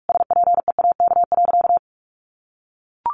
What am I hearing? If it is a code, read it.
Morse code, heard as a keyed tone, about 44 wpm, 700 Hz. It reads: SPEAK.
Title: Sentences from Top 1100 Words 45wpm